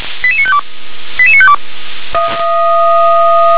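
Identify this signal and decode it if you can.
Five-tone selective call (ZVEI), heard as quick runs of five tones, tones of about 70 ms. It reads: 88952 88952
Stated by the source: ZVEI five-tone selective call received by SDR.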